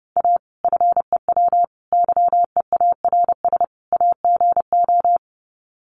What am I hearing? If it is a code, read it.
Morse code, heard as a keyed tone, about 30 wpm, 700 Hz. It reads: A FEW YEARS AGO
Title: Sets of 4 Words 30wpm